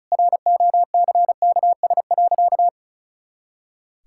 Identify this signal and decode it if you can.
Morse code, heard as a keyed tone, about 34 wpm, 700 Hz. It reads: ROCKS.
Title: Pangram Challenge v2 - 35wpm